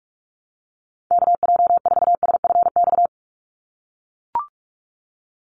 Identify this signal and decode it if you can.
Morse code, heard as a keyed tone, about 44 wpm, 700 Hz. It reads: KJ4SFX
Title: USA Call Signs 45wpm